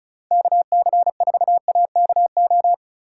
Morse code, heard as a keyed tone, about 34 wpm, 690 Hz. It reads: KC4AKO
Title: USA Call Signs - 6 Characters - 2x Character Spacing - 35wpm